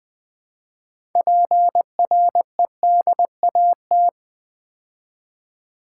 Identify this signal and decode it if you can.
Morse code, heard as a keyed tone, about 20 wpm, 690 Hz. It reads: PREDAT
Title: Build A Word - 3 Letters At A Time - 8-Letter Words - 20wpm